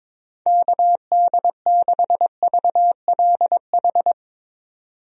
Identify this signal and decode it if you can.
Morse code, heard as a keyed tone, about 22 wpm, 700 Hz. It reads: KD6VLH
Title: USA Call Signs - 6 Characters - 22wpm